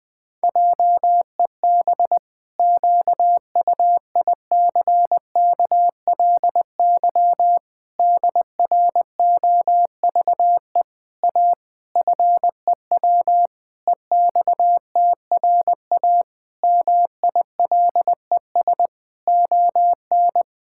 Morse code, heard as a keyed tone, about 20 wpm, 700 Hz. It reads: JEB QUICKLY DROVE A FEW EXTRA MILES ON
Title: Pangram Challenge v2 - 20wpm